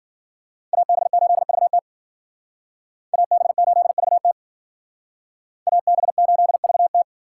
Morse code, heard as a keyed tone, about 51 wpm, 700 Hz. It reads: AB8VT AB8VT AB8VT